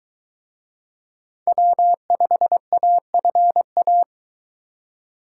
Morse code, heard as a keyed tone, about 23 wpm, 700 Hz. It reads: W5AFA